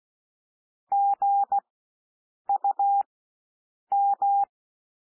Morse code, heard as a keyed tone, about 16 wpm, spaced as for 7 wpm, 800 Hz. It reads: GUM
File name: SOUND WORDS 16 wpm